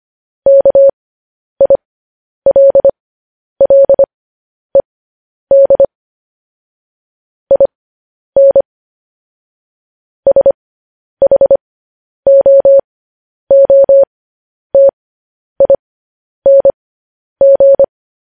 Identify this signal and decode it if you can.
Morse code, heard as a keyed tone, about 25 wpm, spaced as for 10 wpm, 550 Hz. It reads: KILLED IN SHOOTING
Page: news headlines in morse code at 10 wpm